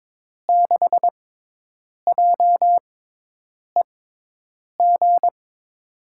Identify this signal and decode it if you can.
Morse code, heard as a keyed tone, about 22 wpm, spaced as for 8 wpm, 700 Hz. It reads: 6JEG
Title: USA Call Signs - 6 Characters - 6x Character Spacing - 22wpm